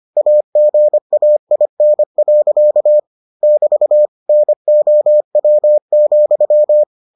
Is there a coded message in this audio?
Yes: Morse code from 0.2 to 6.8 s, heard as a keyed tone, about 25 wpm, 600 Hz. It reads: AGAIN. <BT> NOW,